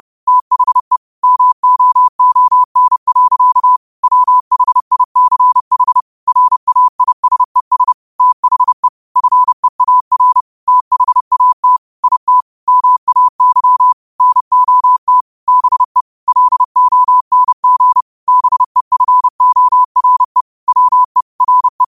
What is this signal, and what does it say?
Morse code, heard as a keyed tone, about 30 wpm, 1000 Hz. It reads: THE MOON. WHICH RAISES THE FEAR THAT IT MAY NOT BE LONG BEFORE WE RE
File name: qotdfiles / 30wpm